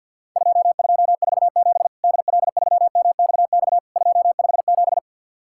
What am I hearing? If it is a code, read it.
Morse code, heard as a keyed tone, about 50 wpm, 700 Hz. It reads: 2247 DL3MXX 257
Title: Worked All Europe DX Contest - QTC Exchange with Repeats - 50wpm